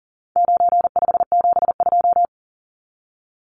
Morse code, heard as a keyed tone, about 39 wpm, 700 Hz. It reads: 9572
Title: Numbers - 4-Digit 40wpm